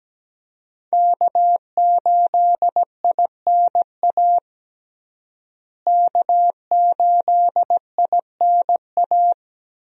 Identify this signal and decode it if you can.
Morse code, heard as a keyed tone, about 17 wpm, 700 Hz. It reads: K8INA K8INA
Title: USA Call Signs - 5 Characters - Repeated 2x - 17wpm